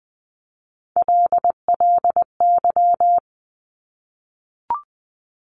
Morse code, heard as a keyed tone, about 19 wpm, 700 Hz. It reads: LLY
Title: Top Three Letter Combinations in Words 20wpm